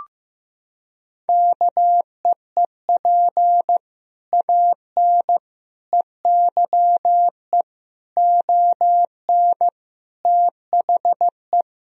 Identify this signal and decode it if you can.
Morse code, heard as a keyed tone, about 15 wpm, 700 Hz. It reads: KEEP AN EYE ON THE
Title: Sentences from Top 1000 Words - Review Part 3 - 15wpm